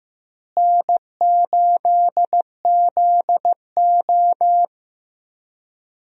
Morse code, heard as a keyed tone, about 15 wpm, 700 Hz. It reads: N8ZO
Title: USA Call Signs - 4 Characters - Repeated 2x - 15wpm